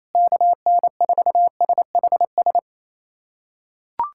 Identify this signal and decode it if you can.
Morse code, heard as a keyed tone, about 28 wpm, 700 Hz. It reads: KN4SHS
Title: USA Callsign DE Callsign 28wpm